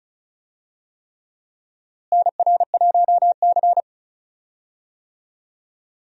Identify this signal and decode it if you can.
Morse code, heard as a keyed tone, about 36 wpm, 700 Hz. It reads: NR1C
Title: USA Call Signs - 4 Characters - Repeated 3x - 35wpm